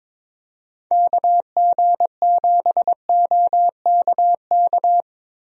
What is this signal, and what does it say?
Morse code, heard as a keyed tone, about 22 wpm, 700 Hz. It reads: KG7OKK
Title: USA Call Signs - 6 Characters - Repeated 2x - 22wpm